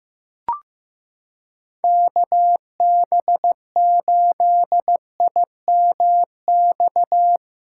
Morse code, heard as a keyed tone, about 16 wpm, 700 Hz. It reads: KB8IMX
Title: USA Callsign DE Callsign 15wpm